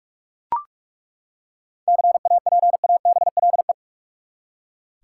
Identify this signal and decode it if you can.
Morse code, heard as a keyed tone, about 43 wpm, 700 Hz. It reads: CAPABLE
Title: Seven Letter Words - 45wpm